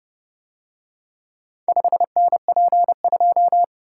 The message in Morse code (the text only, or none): 5NP2